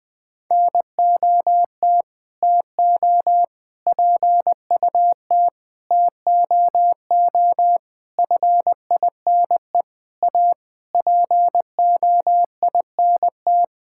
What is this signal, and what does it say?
Morse code, heard as a keyed tone, about 20 wpm, 700 Hz. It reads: NOT TO PUT TOO FINE A POINT
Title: Sentences with 9-Word Phrases - 20wpm